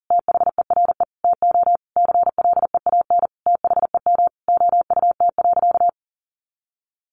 Morse code, heard as a keyed tone, about 40 wpm, 700 Hz. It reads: THERE TO CLEAN THEM OUT.
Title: Sentences from Top 1500 Words 40wpm